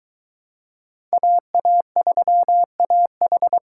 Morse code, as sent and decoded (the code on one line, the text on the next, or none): .- .- ...-- .- ....
AA3AH